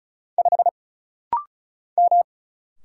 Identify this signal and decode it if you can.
Morse code, heard as a keyed tone, about 36 wpm, 700 Hz. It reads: 5 M
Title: Single Letter-Number Alt - Rapid-Fire 35wpm